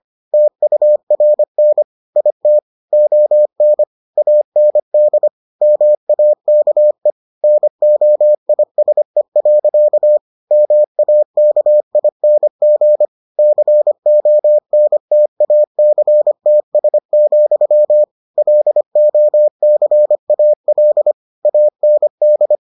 Morse code, heard as a keyed tone, about 25 wpm, 600 Hz. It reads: TURN IT ON AND MAKE NOISE. MAKING CONTACTS, LOCAL AND